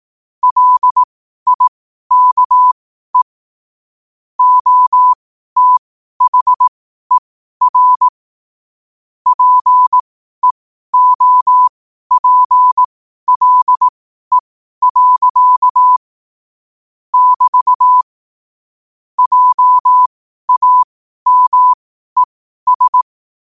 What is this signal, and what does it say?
Morse code, heard as a keyed tone, about 18 wpm, spaced as for 12 wpm, 1000 Hz. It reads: LIKE OTHER PEOPLE. <BT> JAMES
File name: qotdfiles / 12wpm